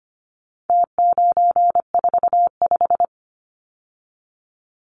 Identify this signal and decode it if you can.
Morse code, heard as a keyed tone, about 25 wpm, 700 Hz. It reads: T945
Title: Numbers - Contest Style - 4-Digit 25wpm